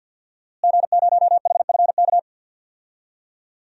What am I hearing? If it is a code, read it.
Morse code, heard as a keyed tone, about 52 wpm, 700 Hz. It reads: G0HFK
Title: DXCC-100 Call Signs - 5 Characters - Repeated 3x - 50wpm